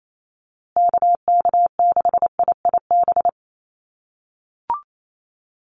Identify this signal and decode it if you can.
Morse code, heard as a keyed tone, about 28 wpm, 700 Hz. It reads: KK6IIB
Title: USA Call Signs Repeated 2x 28wpm